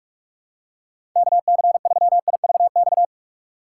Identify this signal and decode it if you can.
Morse code, heard as a keyed tone, about 42 wpm, 700 Hz. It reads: KC3IVX
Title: USA Call Signs - 6 Characters - 6x Character Spacing - 45wpm